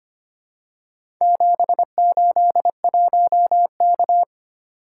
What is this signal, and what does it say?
Morse code, heard as a keyed tone, about 25 wpm, 700 Hz. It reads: 781K
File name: practice / 25wpm / Four Letter-Number Combinations - 25wpm